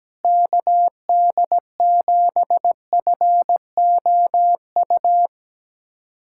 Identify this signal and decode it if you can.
Morse code, heard as a keyed tone, about 17 wpm, 700 Hz. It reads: KD7FOU
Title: USA Callsign DE Callsign 17wpm